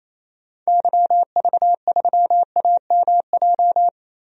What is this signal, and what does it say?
Morse code, heard as a keyed tone, about 28 wpm, 700 Hz. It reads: YV3AMJ
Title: DXCC-100 Call Signs - 6 Characters - 28wpm